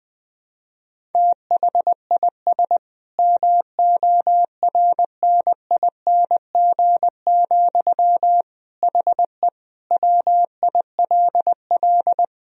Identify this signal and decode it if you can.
Morse code, heard as a keyed tone, about 20 wpm, 700 Hz. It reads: THIS MORNING, HE WILL
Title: Sentences from Top 1000 Words - Review Part 1 - 20wpm